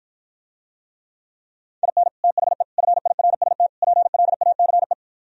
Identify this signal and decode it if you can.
Morse code, heard as a keyed tone, about 55 wpm, 700 Hz. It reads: IN THE FIRST PLACE